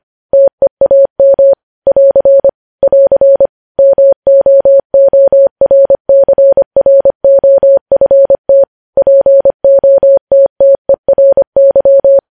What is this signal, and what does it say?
Morse code, heard as a keyed tone, about 25 wpm, 550 Hz. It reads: TEAM <AR> <AR> MOORCROFT POTTERY
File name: podcast / News headlines-1x-25x25